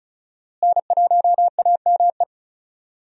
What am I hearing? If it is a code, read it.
Morse code, heard as a keyed tone, about 36 wpm, 700 Hz. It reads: N1AME
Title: USA Call Signs - 5 Characters - Repeated 2x - 35wpm